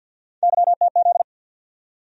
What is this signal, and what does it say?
Morse code, heard as a keyed tone, about 47 wpm, 700 Hz. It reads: YTZ
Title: USA Call Sign Postfixes - 3 Letters 50wpm